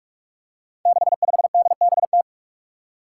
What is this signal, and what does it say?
Morse code, heard as a keyed tone, about 46 wpm, 700 Hz. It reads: 65DBT
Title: Five Challenging Letter-Number Combinations 45wpm